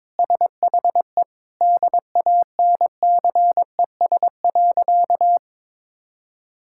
Morse code, heard as a keyed tone, about 22 wpm, 700 Hz. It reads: SHE DANCES.
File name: practice / 22wpm / Sentences with 3-Word Phrases - 22wpm